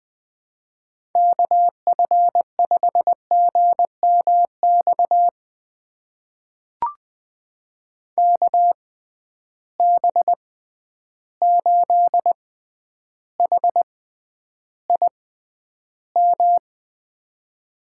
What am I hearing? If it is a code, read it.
Morse code, heard as a keyed tone, about 20 wpm, spaced as for 10 wpm, 700 Hz. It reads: KF5GMX KB8HIM
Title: USA Call Signs - 6 Characters - 6x Character Spacing - 20wpm